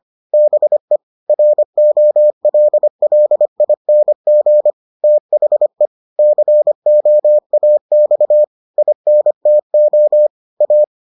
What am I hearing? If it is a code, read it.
Morse code, heard as a keyed tone, about 25 wpm, 600 Hz. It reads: BE ROLLING THE COAX INTO A